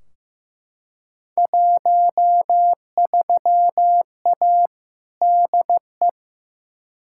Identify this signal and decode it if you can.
Morse code, heard as a keyed tone, about 16 wpm, 700 Hz. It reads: 13A DE
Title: Field Day Exchanges Encore - 15wpm